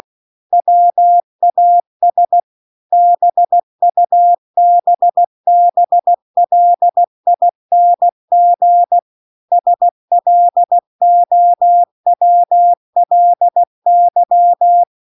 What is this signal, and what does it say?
Morse code, heard as a keyed tone, about 16 wpm, 700 Hz. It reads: WAS BUBBLING SLOWLY